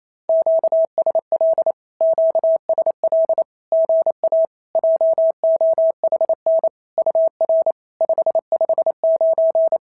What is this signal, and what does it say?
Morse code, heard as a keyed tone, about 28 wpm, 650 Hz. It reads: QSL QSL GA JOHN UR 559
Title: SOTA QSO Practice 28wpm